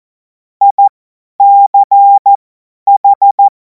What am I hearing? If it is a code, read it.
Morse code, heard as a keyed tone, about 14 wpm, spaced as for 10 wpm, 800 Hz. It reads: ICH
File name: Morsekurs / Witze / Witz 10. Februar 2026 50 F70